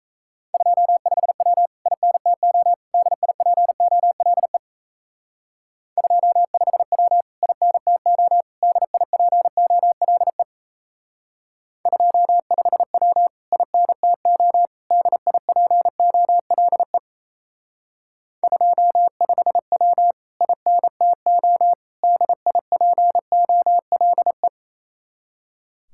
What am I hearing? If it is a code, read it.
Morse code, heard as a keyed tone, about 34 wpm, 700 Hz. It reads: 25W INTO DIPOLE 25W INTO DIPOLE 25W INTO DIPOLE 25W INTO DIPOLE
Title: Speed-Racing Common QSO Phrases - 28wpm